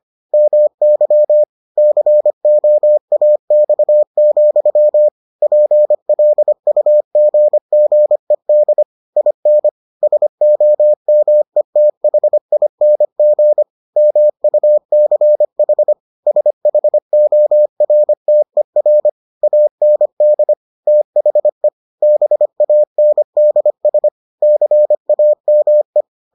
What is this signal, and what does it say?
Morse code, heard as a keyed tone, about 25 wpm, 600 Hz. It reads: MY COAX, PLUGGED IN SOMETHING MUCH SHORTER AND THE BANDS CAME